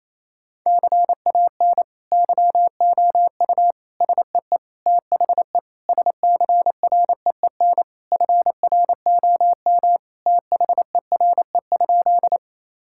Morse code, heard as a keyed tone, about 28 wpm, 700 Hz. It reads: CAN YOU SEE THE SCREEN FROM THERE?
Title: Sentences from Top 1000 Words - Review Part 6 - 28wpm